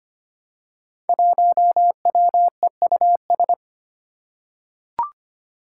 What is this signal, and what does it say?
Morse code, heard as a keyed tone, about 25 wpm, 700 Hz. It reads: 1WEUS
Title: Five Challenging Letter-Number Combinations 25wpm